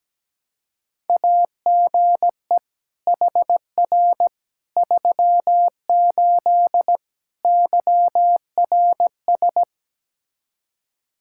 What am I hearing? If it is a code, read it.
Morse code, heard as a keyed tone, about 17 wpm, 700 Hz. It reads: AGE HR 38 YRS
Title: Practice QSO - Exchange 3 - More Info - 17wpm